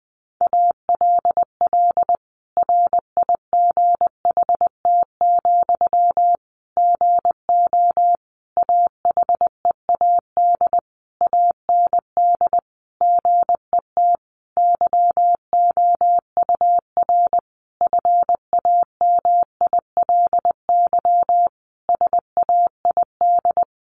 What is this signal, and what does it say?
Morse code, heard as a keyed tone, about 19 wpm, 700 Hz. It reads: ALL RIGHT, GO AHEAD AND GET YOUR FAMILY SAID
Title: Sentences from Top 1100 Words 20wpm